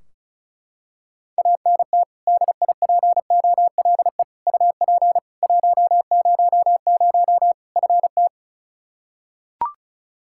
Morse code, heard as a keyed tone, about 34 wpm, 700 Hz. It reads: ANT DIPOLE UP 100 FT
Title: Practice QSO - Exchange 2 - Info - 35wpm